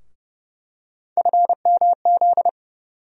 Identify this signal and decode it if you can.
Morse code, heard as a keyed tone, about 30 wpm, 700 Hz. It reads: FMZ
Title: USA Call Sign Postfixes - 3 Letters 30wpm